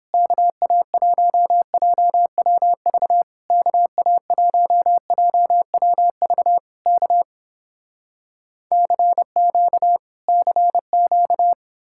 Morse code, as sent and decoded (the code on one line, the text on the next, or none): -.- .- .---- .--- .-- ...- / -.- .- .---- .--- .-- ...- / -.- / -.-. --.- / -.-. --.-
KA1JWV KA1JWV K CQ CQ